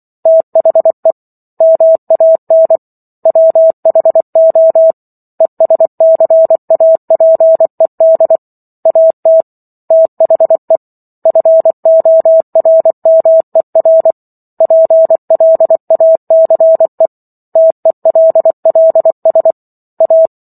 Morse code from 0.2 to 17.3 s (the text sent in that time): THE MAN WHO ESCAPED AT THE FORMER PLACE